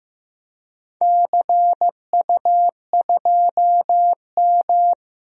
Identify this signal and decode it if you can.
Morse code, heard as a keyed tone, about 15 wpm, 700 Hz. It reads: CU2M
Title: DXCC-100 Call Signs - 4 Characters - Repeated 2x - 15wpm